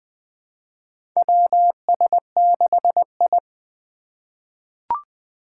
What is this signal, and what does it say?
Morse code, heard as a keyed tone, about 20 wpm, 700 Hz. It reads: WS6I
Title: USA Call Signs - 4 Characters - 2x Character Spacing - 20wpm